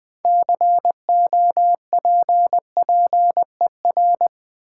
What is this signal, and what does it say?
Morse code, heard as a keyed tone, about 20 wpm, 690 Hz. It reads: COPPER
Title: Build A Word - 2 Letters At A Time - 6-Letter Words - 20wpm